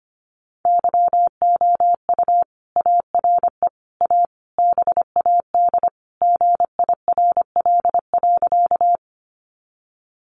Morse code, heard as a keyed tone, about 25 wpm, 700 Hz. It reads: YOU ARE A BAD GIRL.
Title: Sentences from Top 700 Words 25wpm